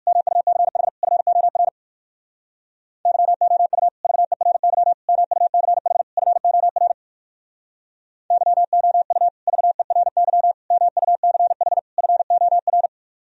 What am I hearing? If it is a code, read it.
Morse code, heard as a keyed tone, about 58 wpm, 700 Hz. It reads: MUCH FOR YOU VERY MUCH FOR YOU VERY MUCH FOR